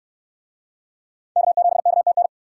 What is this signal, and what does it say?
Morse code, heard as a keyed tone, about 59 wpm, 700 Hz. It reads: K7KTN